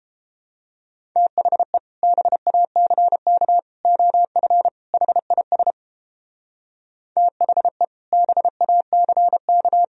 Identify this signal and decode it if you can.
Morse code, heard as a keyed tone, about 31 wpm, 700 Hz. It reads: THE BACK OF HIS THE BACK